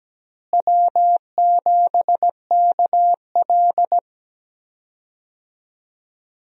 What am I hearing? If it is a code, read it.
Morse code, heard as a keyed tone, about 17 wpm, 700 Hz. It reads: W7KL
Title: USA Call Signs - 4 Characters - Repeated 2x - 17wpm